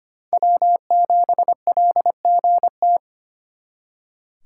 Morse code, heard as a keyed tone, about 32 wpm, 700 Hz. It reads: W7LGT